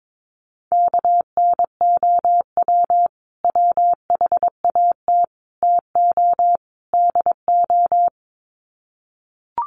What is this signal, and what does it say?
Morse code, heard as a keyed tone, about 22 wpm, 700 Hz. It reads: KNOW WHAT TO DO
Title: Sets of 4 Words 22wpm